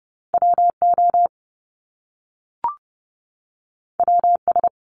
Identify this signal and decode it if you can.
Morse code, heard as a keyed tone, about 30 wpm, 700 Hz. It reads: WO WS
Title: Top Two Letter Combinations in Words 30wpm